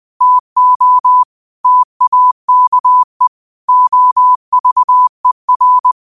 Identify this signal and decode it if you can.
Morse code, heard as a keyed tone, about 20 wpm, 1000 Hz. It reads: TO TAKE OVER
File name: qotdfiles / 20wpm